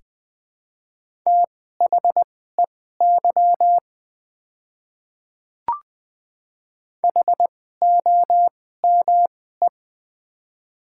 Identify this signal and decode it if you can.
Morse code, heard as a keyed tone, about 20 wpm, spaced as for 14 wpm, 700 Hz. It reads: THEY HOME
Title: 2x Character Spacing - Top 200 Words - 20wpm